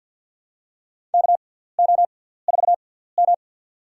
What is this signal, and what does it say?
Morse code, heard as a keyed tone, about 53 wpm, spaced as for 17 wpm, 700 Hz. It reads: KO4M
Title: USA Call Signs - 6x Character Spacing - 50wpm